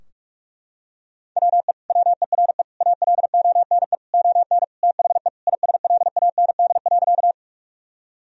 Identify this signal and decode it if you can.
Morse code, heard as a keyed tone, about 41 wpm, 700 Hz. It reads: WE WERE ALONE ON THE ISLAND.
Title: Sentences from Top 1000 Words - Review Part 3 - 45wpm